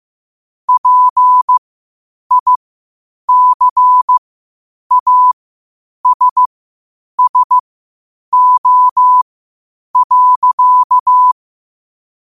Morse code, heard as a keyed tone, about 15 wpm, spaced as for 8 wpm, 1000 Hz. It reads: PICASSO.
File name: qotdfiles / 8wpm